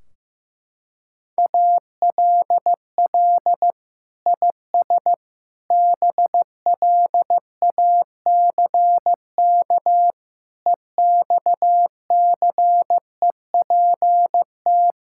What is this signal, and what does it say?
Morse code, heard as a keyed tone, about 15 wpm, 700 Hz. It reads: ALL IS BLACK EXCEPT
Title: Sentences from Top 1600 Words 15wpm